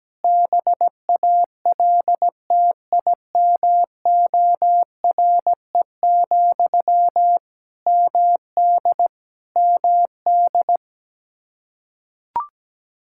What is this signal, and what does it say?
Morse code, heard as a keyed tone, about 17 wpm, 700 Hz. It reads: BALTIMORE, MD MD
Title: Practice QSO - Exchange 1 - Beginning - 17wpm